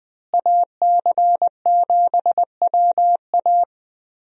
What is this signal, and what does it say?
Morse code, heard as a keyed tone, about 21 wpm, 700 Hz. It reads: AC7WA
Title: USA Call Signs - 5 Characters - Repeated 2x - 20wpm